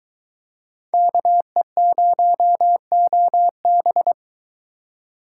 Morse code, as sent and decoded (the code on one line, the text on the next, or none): -.- . ----- --- -...
KE0OB